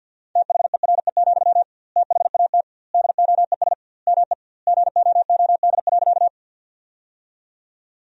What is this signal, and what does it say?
Morse code, heard as a keyed tone, about 51 wpm, 700 Hz. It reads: THERE, THAT DOES ME GOOD.
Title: 2x Word-Spacing - Sentences from Top 200 Words - 50wpm